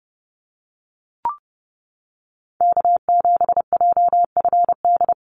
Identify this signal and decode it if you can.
Morse code, heard as a keyed tone, about 30 wpm, 700 Hz. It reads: K7JFD
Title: NA SOTA Chaser Callsigns 30wpm